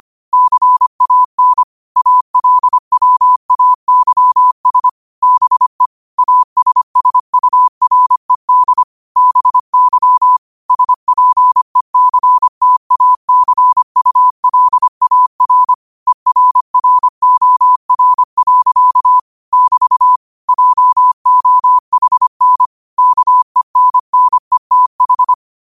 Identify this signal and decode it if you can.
Morse code, heard as a keyed tone, about 25 wpm, 1000 Hz. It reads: CAN ALWAYS BE ASSURED BY SPECTACULAR ERROR. <BT> JOHN KENNETH
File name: qotdfiles / 25wpm